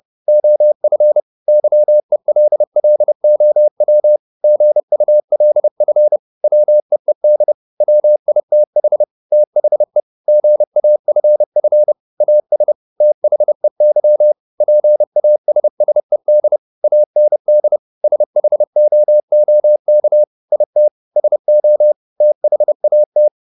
Morse code, heard as a keyed tone, about 30 wpm, 600 Hz. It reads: OF YELLOW GULF WEED WITH THE GAFF AS THEY PASSED AND SHOOK IT SO THAT